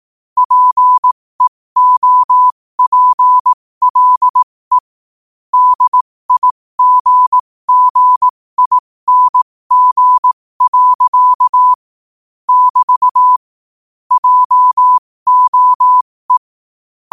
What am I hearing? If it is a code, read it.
Morse code, heard as a keyed tone, about 18 wpm, spaced as for 15 wpm, 1000 Hz. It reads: PEOPLE DIGGING. <BT> JOE
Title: 15 WPM morse code quotes for Thu, 29 Feb 2024